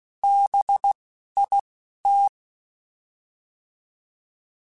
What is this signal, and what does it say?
Morse code, heard as a keyed tone, about 16 wpm, spaced as for 11 wpm, 800 Hz. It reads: BIT
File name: Alphabetical / Alphabetical mp3s / Bs @10